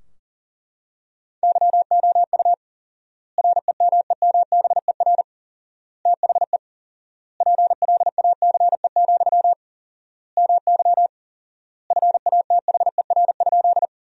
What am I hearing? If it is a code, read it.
Morse code, heard as a keyed tone, about 37 wpm, 700 Hz. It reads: YOU REMEMBER THE PLACE, MY FATHER?
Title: 4x Word-Spacing - Sentences from Top 400 Words - 40wpm